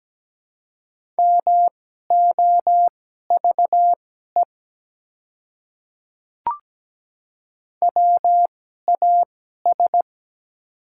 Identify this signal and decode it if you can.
Morse code, heard as a keyed tone, about 17 wpm, spaced as for 12 wpm, 700 Hz. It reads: MOVE WAS
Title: 2x Character Spacing - Top 200 Words - 17wpm